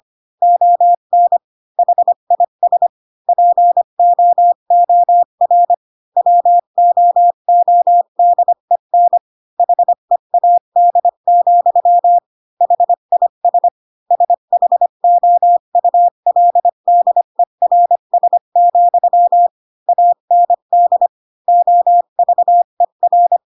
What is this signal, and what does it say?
Morse code, heard as a keyed tone, about 25 wpm, 700 Hz. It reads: ON HIS POOR WOODEN HEAD, HIS SHOULDERS, AND OVER